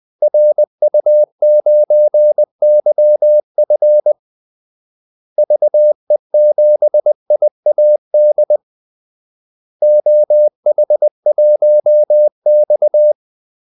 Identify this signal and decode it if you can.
Morse code, heard as a keyed tone, about 20 wpm, 600 Hz. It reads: RU9YF VE7IAD OH1X
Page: Kaldesignaler 20 wpm